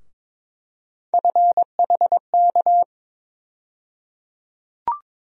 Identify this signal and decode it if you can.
Morse code, heard as a keyed tone, about 22 wpm, 700 Hz. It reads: FHK
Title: USA Call Sign Postfixes - 3 Letters 22wpm